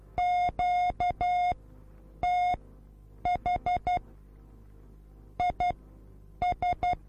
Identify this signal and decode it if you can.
Morse code, heard as a keyed tone, about 12 wpm, spaced as for 8 wpm, 720 Hz. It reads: QTH IS